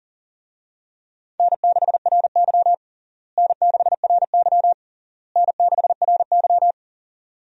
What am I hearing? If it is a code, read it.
Morse code, heard as a keyed tone, about 40 wpm, 700 Hz. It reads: N6RY N6RY N6RY
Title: USA Call Signs - 4 Characters - Repeated 3x - 40wpm